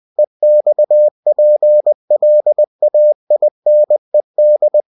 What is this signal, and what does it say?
Morse code, heard as a keyed tone, about 20 wpm, 600 Hz. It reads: EXPLAINED
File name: cw-books / winnie-the-pooh / winnie-20wpm-page-0096